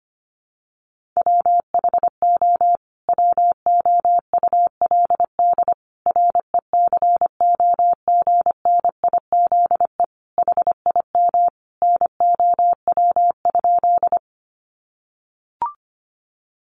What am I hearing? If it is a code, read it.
Morse code, heard as a keyed tone, about 25 wpm, 700 Hz. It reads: WHO WOULD RECOGNIZE HIM NOW?
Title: Sentences from Top 1400 Words 25wpm